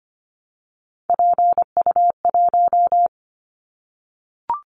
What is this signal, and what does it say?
Morse code, heard as a keyed tone, about 25 wpm, 700 Hz. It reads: PU1
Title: Callsign Prefixes for Top 50 DXCC - 25wpm